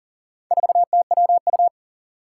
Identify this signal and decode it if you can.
Morse code, heard as a keyed tone, about 37 wpm, 700 Hz. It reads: 4TWU
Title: Four Challenging Letter-Number Combinations 40wpm